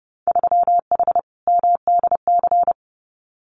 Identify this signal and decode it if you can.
Morse code, heard as a keyed tone, about 30 wpm, 700 Hz. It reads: 3H MDC
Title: Winter Field Day Exchanges Encore 30wpm